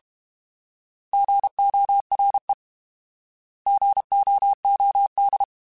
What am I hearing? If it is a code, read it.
Morse code, heard as a keyed tone, about 32 wpm, 800 Hz. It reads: GORE GOOD
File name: Alphabetical / Alphabetical mp3s / Gs @30